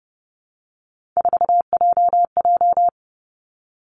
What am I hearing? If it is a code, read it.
Morse code, heard as a keyed tone, about 30 wpm, 700 Hz. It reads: 4JJ
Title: Three Challenging Letter-Number Combinations 30wpm